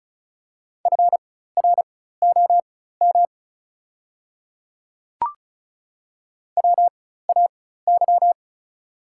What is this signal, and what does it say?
Morse code, heard as a keyed tone, about 34 wpm, spaced as for 16 wpm, 700 Hz. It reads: FROM WAY
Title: 4x Character Spacing - Top 100 Words - 35wpm